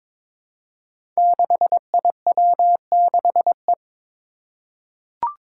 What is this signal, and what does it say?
Morse code, heard as a keyed tone, about 22 wpm, 700 Hz. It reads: 6IW6E
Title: Five Challenging Letter-Number Combinations 22wpm